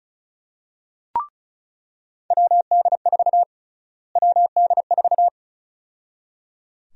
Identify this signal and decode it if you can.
Morse code, heard as a keyed tone, about 36 wpm, 700 Hz. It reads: WD4 WD4
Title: USA Call Sign Prefixes - Repeated 2x - 35wpm